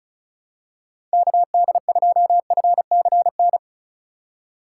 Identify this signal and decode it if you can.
Morse code, heard as a keyed tone, about 36 wpm, 700 Hz. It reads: KD2FCN
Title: USA Call Signs - 6 Characters - 2x Character Spacing - 35wpm